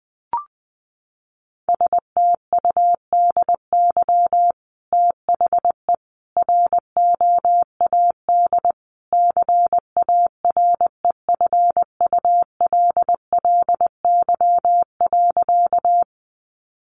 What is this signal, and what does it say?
Morse code, heard as a keyed tone, about 19 wpm, 700 Hz. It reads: STUDY THE ROAD CAREFULLY.
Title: Sentences from Top 1300 Words 20wpm